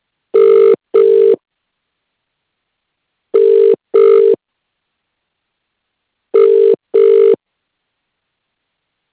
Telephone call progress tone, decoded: ringback tone